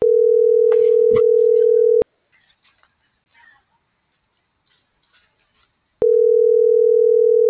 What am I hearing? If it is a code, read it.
Ringback tone (North America), 440 + 480 Hz.
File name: CENTRAL GOMSA / out lizeth.morales